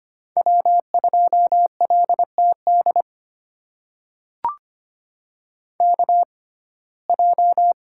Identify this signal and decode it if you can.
Morse code, heard as a keyed tone, about 25 wpm, spaced as for 18 wpm, 700 Hz. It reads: W2LTD KJ
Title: USA Call Signs - 5 Characters - 6x Character Spacing - 25wpm